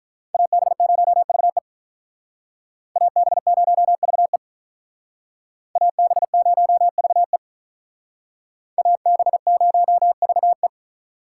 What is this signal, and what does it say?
Morse code, heard as a keyed tone, about 44 wpm, 700 Hz. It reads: AB0VE AB0VE AB0VE AB0VE